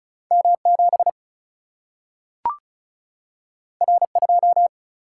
Morse code, heard as a keyed tone, about 36 wpm, 700 Hz. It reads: M 7 R 2
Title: Single-Letter Single-Number 35wpm